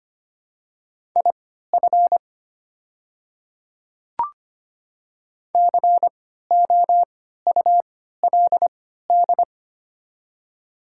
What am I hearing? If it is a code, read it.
Morse code, heard as a keyed tone, about 25 wpm, spaced as for 14 wpm, 690 Hz. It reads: IF COULD
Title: 3x Character Spacing - Top 300 Words - 25wpm